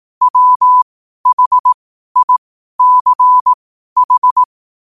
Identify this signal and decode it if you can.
Morse code, heard as a keyed tone, about 18 wpm, spaced as for 12 wpm, 1000 Hz. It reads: WHICH
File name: qotdfiles / 12wpm